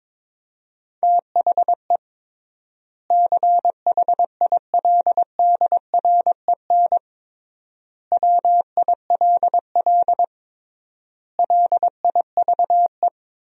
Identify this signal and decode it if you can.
Morse code, heard as a keyed tone, about 22 wpm, 700 Hz. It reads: THE CHILDREN WILL LIVE
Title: 3x Word-Spacing - Sentences from Top 300 Words - 22wpm